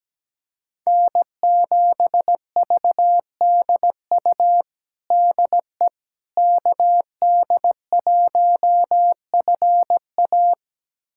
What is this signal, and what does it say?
Morse code, heard as a keyed tone, about 17 wpm, 700 Hz. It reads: N7VDU DE KD1FA
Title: USA Callsign DE Callsign - 5 Characters - 17wpm